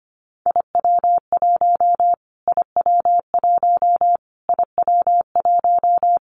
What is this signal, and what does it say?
Morse code, heard as a keyed tone, about 25 wpm, 700 Hz. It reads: IW1 IW1 IW1
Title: Callsign Prefixes for Top 25 DXCC - Repeated 3x - 25wpm